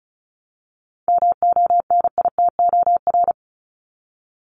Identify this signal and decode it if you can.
Morse code, heard as a keyed tone, about 34 wpm, 700 Hz. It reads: MONITOR
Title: Top 1751-2000 Words 35wpm